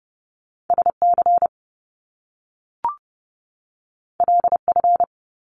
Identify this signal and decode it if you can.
Morse code, heard as a keyed tone, about 30 wpm, 700 Hz. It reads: SC LF
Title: Top Two Letter Combinations in Words 30wpm